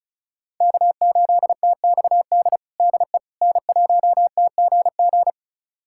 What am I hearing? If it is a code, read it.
Morse code, heard as a keyed tone, about 36 wpm, 700 Hz. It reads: K8TXD DE N1TGG
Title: USA Callsign DE Callsign - 5 Characters - 35wpm